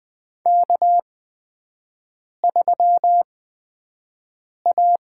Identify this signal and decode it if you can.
Morse code, heard as a keyed tone, about 20 wpm, spaced as for 5 wpm, 700 Hz. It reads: K3A
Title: USA Call Signs - 8x Character Spacing - 20wpm